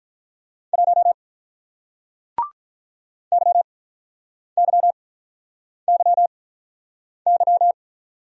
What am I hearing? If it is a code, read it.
Morse code, heard as a keyed tone, about 48 wpm, spaced as for 9 wpm, 700 Hz. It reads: 1 YYYY